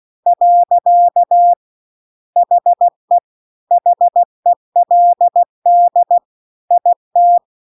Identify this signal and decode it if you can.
Morse code, heard as a keyed tone, about 16 wpm, 700 Hz. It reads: . HE HELD IT
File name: CW-The-Big-Twoheaded-River-2-16WPM